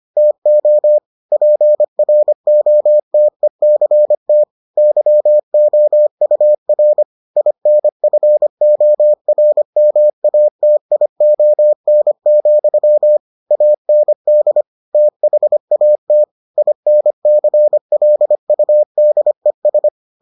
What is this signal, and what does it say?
Morse code, heard as a keyed tone, about 25 wpm, 600 Hz. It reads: TO PROTECT YOUR INFORMATION, AND THAT INCLUDES